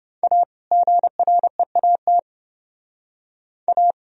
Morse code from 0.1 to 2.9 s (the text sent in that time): A GREAT